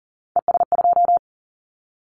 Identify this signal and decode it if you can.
Morse code, heard as a keyed tone, about 42 wpm, 700 Hz. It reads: ES2
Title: Callsign Prefixes for Top 100 DXCC - 40wpm